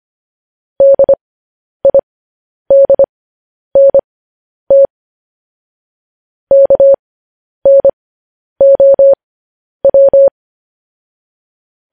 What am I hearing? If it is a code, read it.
Morse code, heard as a keyed tone, about 25 wpm, spaced as for 10 wpm, 550 Hz. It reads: DIDNT KNOW
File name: podcast / News headlines-1x-25x10